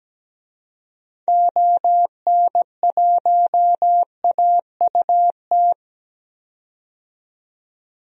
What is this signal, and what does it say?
Morse code, heard as a keyed tone, about 17 wpm, 700 Hz. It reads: ON1AUT
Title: DXCC-100 Call Signs - 6 Characters - 17wpm